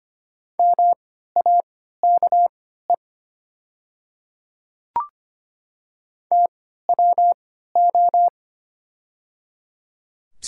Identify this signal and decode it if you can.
Morse code, heard as a keyed tone, about 25 wpm, spaced as for 14 wpm, 700 Hz. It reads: MAKE TWO
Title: 3x Character Spacing - Top 100 Words - 25wpm